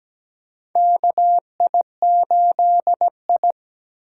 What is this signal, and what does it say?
Morse code, heard as a keyed tone, about 17 wpm, 700 Hz. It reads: KI8I
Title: K1USN SST QSO Practice - 17wpm